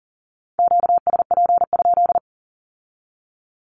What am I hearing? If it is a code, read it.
Morse code, heard as a keyed tone, about 42 wpm, 700 Hz. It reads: QSP?
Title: Q-Codes with Spoken Meaning - 40wpm